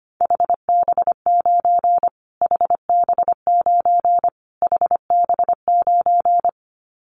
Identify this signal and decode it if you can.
Morse code, heard as a keyed tone, about 25 wpm, 700 Hz. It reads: HB9 HB9 HB9
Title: Callsign Prefixes for Top 25 DXCC - Repeated 3x - 25wpm